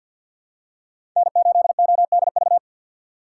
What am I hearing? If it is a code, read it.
Morse code, heard as a keyed tone, about 53 wpm, 700 Hz. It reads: N8ODV
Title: USA Call Signs - 5 Characters - 6x Character Spacing - 50wpm